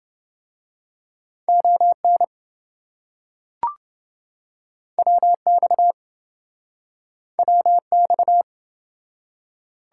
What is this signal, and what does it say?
Morse code, heard as a keyed tone, about 27 wpm, 700 Hz. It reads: ON WX WX